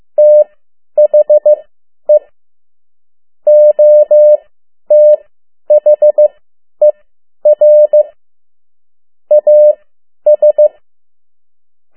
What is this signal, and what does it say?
Morse code, heard as a keyed tone, about 15 wpm, spaced as for 10 wpm, 600 Hz. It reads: THE OTHER AS
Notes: Around the World in 80 Days paragraph one Morse code 10 WPM